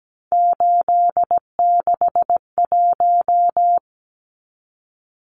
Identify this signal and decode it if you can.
Morse code, heard as a keyed tone, about 17 wpm, 700 Hz. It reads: 861
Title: Numbers - 3-Digit 17wpm